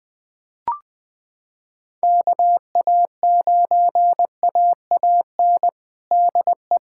Morse code, heard as a keyed tone, about 20 wpm, 700 Hz. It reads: KA9AAN DE
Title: USA Callsign DE Callsign 20wpm